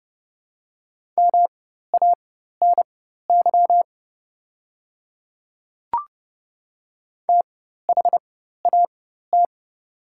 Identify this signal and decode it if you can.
Morse code, heard as a keyed tone, about 30 wpm, spaced as for 14 wpm, 700 Hz. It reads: MANY THAT
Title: 4x Character Spacing - Top 300 Words - 30wpm